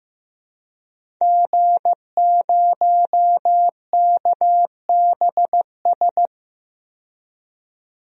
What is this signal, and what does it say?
Morse code, heard as a keyed tone, about 16 wpm, 700 Hz. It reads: G0KBS
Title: DXCC-100 Call Signs - 5 Characters - 15wpm